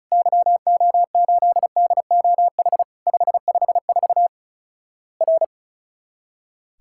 Morse code, heard as a keyed tone, about 36 wpm, 700 Hz. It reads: YO8DOH 554 R
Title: Worked All Europe DX Contest - QTC Exchange - 35wpm